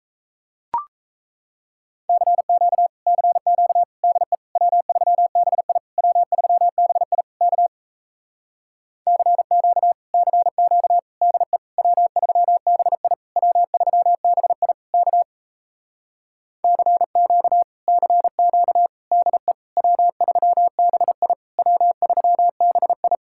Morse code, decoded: CQ CQ DE W3BI W3BI K CQ CQ DE W3BI W3BI K CQ CQ DE W3BI W3BI

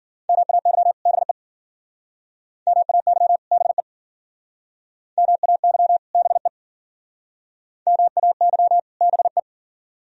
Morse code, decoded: MAY BE MAY BE MAY BE MAY BE

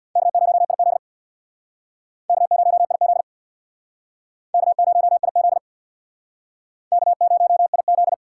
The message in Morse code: K0IZ K0IZ K0IZ K0IZ

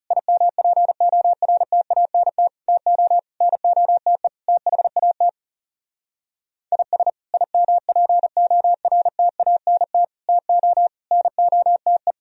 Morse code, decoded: IMPORTANT TO NOTE THAT IS IMPORTANT TO NOTE

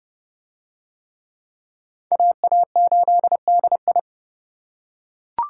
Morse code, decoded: AA8DI

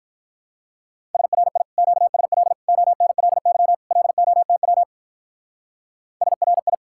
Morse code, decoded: SRI QSL ONLY LOTW SRI

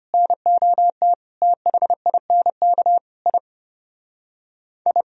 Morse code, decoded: NOT THINK I I

